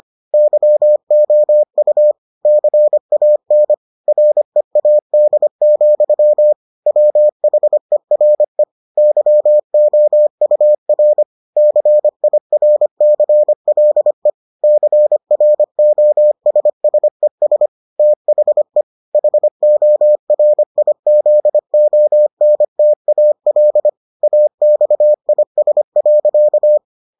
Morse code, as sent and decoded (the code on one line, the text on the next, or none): -.-- --- ..- / -.-. .- -. / .-. .- -.. --..-- / .-- .... .-. / -.-- --- ..- .-. / -.-. .. .-. -.-. .-.. / -.-. .-. --- ... ... ... / - .... / .... --- .-. .. --.. --- -. - .- .-.. / .- -..- .. ... .-.-.-
YOU CAN READ, WHERE YOUR CIRCLE CROSSES THE HORIZONTAL AXIS.